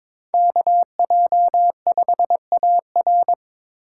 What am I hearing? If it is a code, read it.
Morse code, heard as a keyed tone, about 22 wpm, 700 Hz. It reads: KJ5AR
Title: USA Call Signs - 5 Characters - 2x Character Spacing - 22wpm